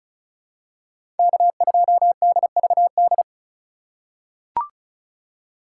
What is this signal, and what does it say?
Morse code, heard as a keyed tone, about 34 wpm, 700 Hz. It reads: K2DVD
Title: USA Call Signs - 5 Characters - 4x Character Spacing - 35wpm